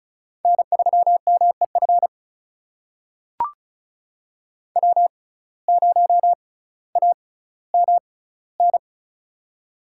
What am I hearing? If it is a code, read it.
Morse code, heard as a keyed tone, about 36 wpm, spaced as for 18 wpm, 700 Hz. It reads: N3MEF W0AMN
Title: USA Call Signs - 5 Characters - 6x Character Spacing - 35wpm